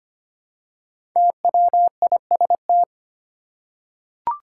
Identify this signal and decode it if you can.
Morse code, heard as a keyed tone, about 25 wpm, 700 Hz. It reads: TWIST
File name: 5 Letter Words - First 3 Letters Given - 25wpm